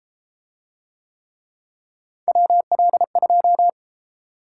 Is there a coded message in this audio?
Yes: Morse code from 2.3 to 3.7 s, heard as a keyed tone, about 33 wpm, 700 Hz. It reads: WL2